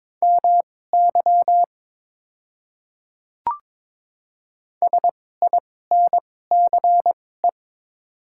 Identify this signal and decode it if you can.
Morse code, heard as a keyed tone, about 22 wpm, spaced as for 16 wpm, 700 Hz. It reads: MY SINCE
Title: 2x Character Spacing - Top 300 Words - 22wpm